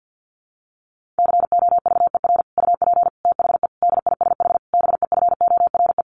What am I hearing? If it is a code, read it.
Morse code, heard as a keyed tone, about 50 wpm, 700 Hz. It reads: COVER UP THE DISH BEFORE
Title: Sentences from Top 1300 Words 50wpm